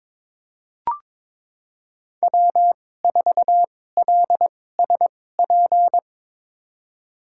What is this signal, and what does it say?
Morse code, heard as a keyed tone, about 22 wpm, spaced as for 16 wpm, 700 Hz. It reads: W4LSP
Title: USA Call Signs - 5 Characters - 2x Character Spacing - 22wpm